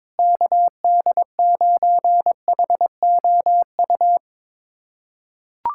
Morse code, heard as a keyed tone, about 22 wpm, 700 Hz. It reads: KD9HOU
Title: USA Callsign DE Callsign - 6 Characters - 22wpm